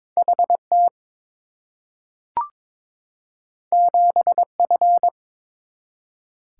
Morse code, heard as a keyed tone, about 23 wpm, 700 Hz. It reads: HT 7F
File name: Two Letter-Number Combinations - 22wpm